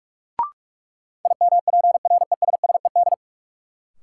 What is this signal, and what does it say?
Morse code, heard as a keyed tone, about 46 wpm, 700 Hz. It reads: IMPRESSED
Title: Sentences from Top 1600 Words - New Word Drill Alt - 45wpm